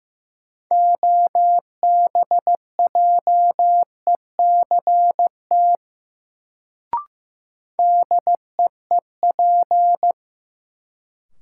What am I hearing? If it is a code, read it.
Morse code, heard as a keyed tone, about 15 wpm, 700 Hz. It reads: OBJECT DEEP
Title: Sentences from Top 500 Words - New Word Drill - 15wpm